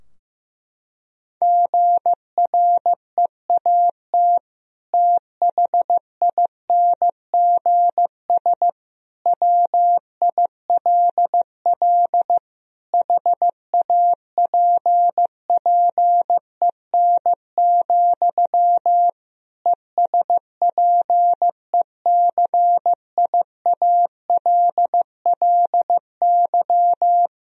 Morse code, heard as a keyed tone, about 15 wpm, 700 Hz. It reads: GREAT THINGS WILL HAPPEN, ESPECIALLY
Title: Sentences from Top 1000 Words - Review Part 4 - 15wpm